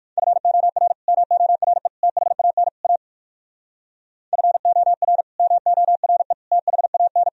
Morse code, heard as a keyed tone, about 64 wpm, spaced as for 56 wpm, 700 Hz. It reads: FOR MORE THAN A FOR MORE THAN
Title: Speed-Racing - Sets of 4 Words 45wpm